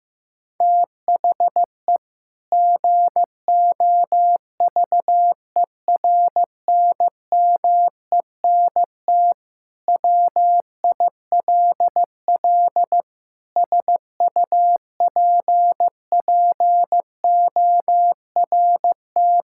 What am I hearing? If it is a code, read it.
Morse code, heard as a keyed tone, about 15 wpm, 700 Hz. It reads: THE GOVERNMENT WILL SUPPORT
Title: Sentences from Top 1000 Words - Review Part 5 - 15wpm